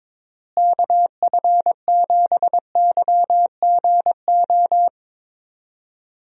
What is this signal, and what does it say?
Morse code, heard as a keyed tone, about 21 wpm, 700 Hz. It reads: KF7YGO
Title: USA Call Signs - 6 Characters - Repeated 2x - 22wpm